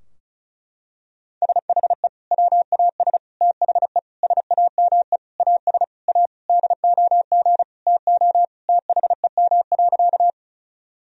Morse code, heard as a keyed tone, about 36 wpm, 700 Hz. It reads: SHE WAS THE SAME AS A DOG TO THEM.
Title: Sentences from Top 1000 Words - Review Part 2 - 35wpm